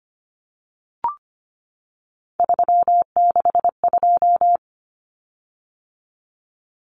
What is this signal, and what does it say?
Morse code, heard as a keyed tone, about 25 wpm, 700 Hz. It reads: 362
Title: Common Cut Numbers - 3-Digit 25wpm